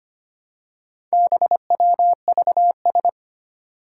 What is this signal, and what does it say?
Morse code, heard as a keyed tone, about 25 wpm, 700 Hz. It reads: BWVS
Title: Four Challenging Letter-Number Combinations 25wpm